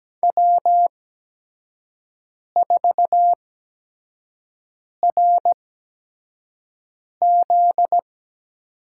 Morse code, heard as a keyed tone, about 17 wpm, spaced as for 5 wpm, 700 Hz. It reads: W4RZ
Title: USA Call Signs - 5 Characters - 8x Character Spacing - 17wpm